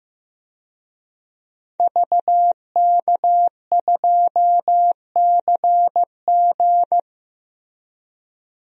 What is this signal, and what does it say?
Morse code, heard as a keyed tone, about 15 wpm, 700 Hz. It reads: VK2CG
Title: DXCC-100 Call Signs - 5 Characters - Repeated 2x - 15wpm